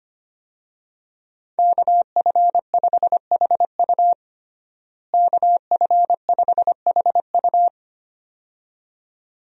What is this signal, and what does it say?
Morse code, heard as a keyed tone, about 25 wpm, 700 Hz. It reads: KF5HU KF5HU
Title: USA Call Signs - 5 Characters - Repeated 2x - 25wpm